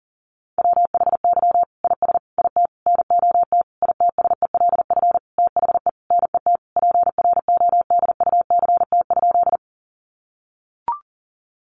Morse code, heard as a keyed tone, about 40 wpm, 700 Hz. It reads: WHY IS IT NOT ITSELF THE NET PRODUCT?
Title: Sentences from Top 1400 Words 40wpm